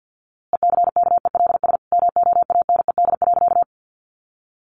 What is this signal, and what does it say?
Morse code, heard as a keyed tone, about 48 wpm, 700 Hz. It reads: ECKELS MOANED.
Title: A Sound Of Thunder 50wpm